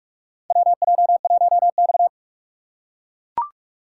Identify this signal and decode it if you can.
Morse code, heard as a keyed tone, about 43 wpm, 700 Hz. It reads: WJ1X
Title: USA Call Signs - 4 Characters - 4x Character Spacing - 45wpm